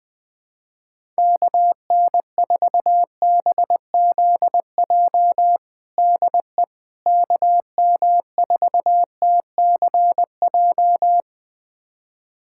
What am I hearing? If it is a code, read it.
Morse code, heard as a keyed tone, about 20 wpm, 700 Hz. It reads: KN4BZJ DE KM4TCJ
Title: USA Callsign DE Callsign - 6 Characters - 20wpm